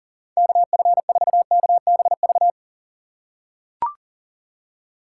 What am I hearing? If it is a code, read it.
Morse code, heard as a keyed tone, about 40 wpm, 700 Hz. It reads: KF4KBV
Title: USA Call Signs - 6 Characters - 2x Character Spacing - 40wpm